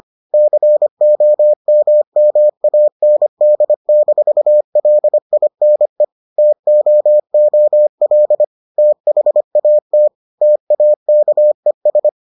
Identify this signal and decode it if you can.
Morse code, heard as a keyed tone, about 25 wpm, 600 Hz. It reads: COMMAND-LINE TOOL THAT TAKES